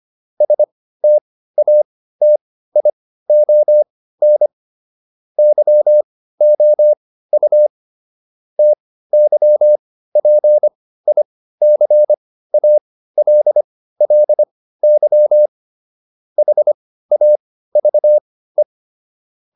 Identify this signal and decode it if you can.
Morse code, heard as a keyed tone, about 25 wpm, spaced as for 15 wpm, 600 Hz. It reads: STATION YOU TYPICALLY HAVE